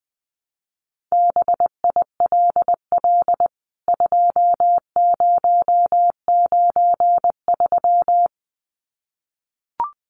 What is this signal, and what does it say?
Morse code, heard as a keyed tone, about 20 wpm, 700 Hz. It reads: BILL 2093
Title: CWT Exchange 20wpm